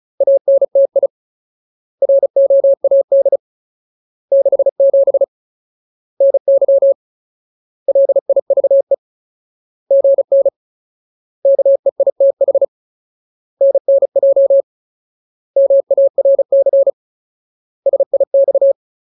Morse code, heard as a keyed tone, about 35 wpm, 550 Hz. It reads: ANTI ROAD 67 NY LIVE GN KEITH NNJ MARC SIX
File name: Ultimate Word List / 35WPM Ultimate